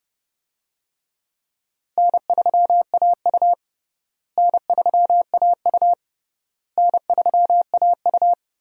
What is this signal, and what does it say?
Morse code, heard as a keyed tone, about 28 wpm, 700 Hz. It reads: N3AU N3AU N3AU
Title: USA Call Signs - 4 Characters - Repeated 3x - 30wpm